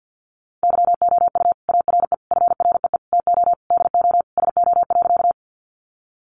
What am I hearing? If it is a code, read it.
Morse code, heard as a keyed tone, about 50 wpm, 700 Hz. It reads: YOU ARE FREE TO DO SO.
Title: Sentences from Top 500 Words 50wpm